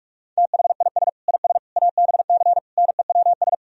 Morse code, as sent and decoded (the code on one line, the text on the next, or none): - .... .. ... / .. ... / .- -... -.-. / -. .-- ...
THIS IS ABC NEWS